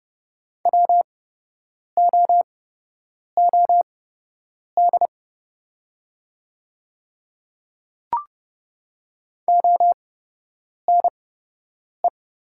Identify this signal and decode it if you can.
Morse code, heard as a keyed tone, about 31 wpm, spaced as for 8 wpm, 700 Hz. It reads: WOOD ONE
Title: 8x Character Spacing - Top 300 Words - 30wpm